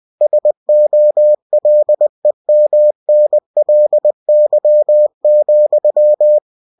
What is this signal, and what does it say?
Morse code, heard as a keyed tone, about 20 wpm, 600 Hz. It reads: SOLEMNLY,
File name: cw-books / winnie-the-pooh / winnie-20wpm-page-0087